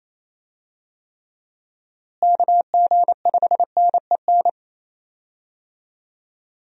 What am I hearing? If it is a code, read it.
Morse code, heard as a keyed tone, about 28 wpm, 700 Hz. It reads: KG5NEN
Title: USA Call Signs - 6 Characters - Repeated 2x - 28wpm